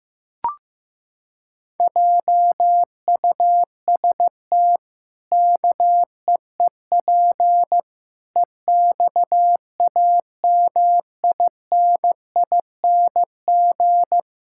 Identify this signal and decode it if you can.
Morse code, heard as a keyed tone, about 15 wpm, 700 Hz. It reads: JUST KEEP EXAMINING
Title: Pangram Challenge v2 - 15wpm